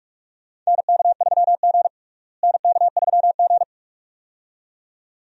Morse code, heard as a keyed tone, about 41 wpm, 700 Hz. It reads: NK3G NK3G
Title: USA Call Signs - 4 Characters - Repeated 2x - 45wpm